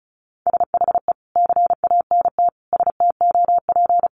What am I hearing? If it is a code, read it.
Morse code, heard as a keyed tone, about 34 wpm, 700 Hz. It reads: SHE CANT STOP
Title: Easy Phrases 35wpm